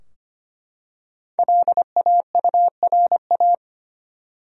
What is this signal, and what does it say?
Morse code, heard as a keyed tone, about 25 wpm, 700 Hz. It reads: LAURA
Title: USA Common Names - 25wpm